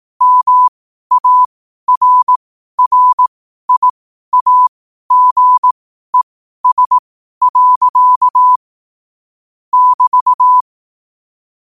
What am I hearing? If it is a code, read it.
Morse code, heard as a keyed tone, about 18 wpm, spaced as for 12 wpm, 1000 Hz. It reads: MARRIAGES. <BT>
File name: qotdfiles / 12wpm